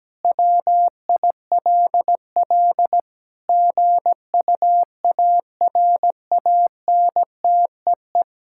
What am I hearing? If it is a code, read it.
Morse code, heard as a keyed tone, about 17 wpm, 700 Hz. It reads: WILL GUARANTEE
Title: Sentences from Top 1600 Words 17wpm